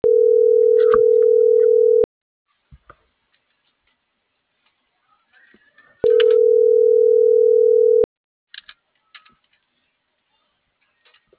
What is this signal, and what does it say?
Ringback tone (North America), 440 + 480 Hz.